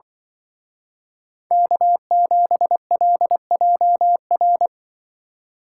Morse code, heard as a keyed tone, about 23 wpm, 700 Hz. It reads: K7LJR
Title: Speed Racing - USA 1x3 Call Signs 20wpm